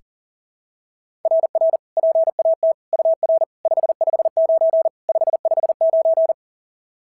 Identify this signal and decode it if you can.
Morse code, heard as a keyed tone, about 40 wpm, 690 Hz. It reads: RR PAT UR 559 559
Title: SOTA QSO Practice 40wpm